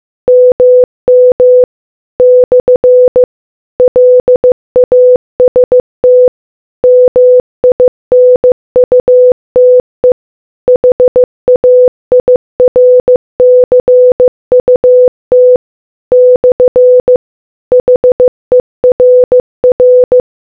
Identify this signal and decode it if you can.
Morse code, heard as a keyed tone, about 15 wpm, 500 Hz. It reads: MM / LAST MINUTE HAIRCUT / HERR